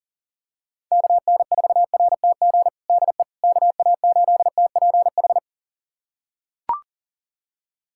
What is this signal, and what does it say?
Morse code, heard as a keyed tone, about 40 wpm, 700 Hz. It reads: KN4RTG DE KA8TPH
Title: USA Callsign DE Callsign - 6 Characters - 40wpm